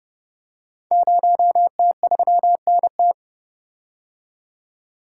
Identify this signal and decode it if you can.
Morse code, heard as a keyed tone, about 30 wpm, 700 Hz. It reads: 0T3NT
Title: Code Groups - 5 Letter:Numbers - 30wpm